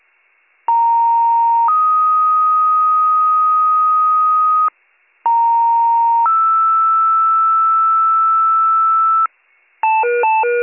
Two-tone page, heard as a one-second tone, then a three-second tone, about 930 then 1250 Hz, about 930 then 1320 Hz. It is heard from 0.7 to 9.3 s.